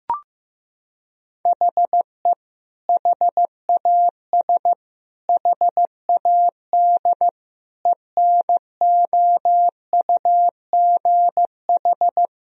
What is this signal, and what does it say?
Morse code, heard as a keyed tone, about 16 wpm, 700 Hz. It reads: HE HAS HAD ENOUGH
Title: Sentences from Top 1000 Words - Review Part 3 - 15wpm